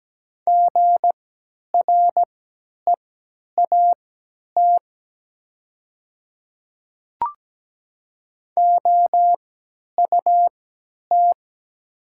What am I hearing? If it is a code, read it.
Morse code, heard as a keyed tone, about 17 wpm, spaced as for 10 wpm, 700 Hz. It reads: GREAT OUT
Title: 3x Character Spacing - Top 200 Words - 17wpm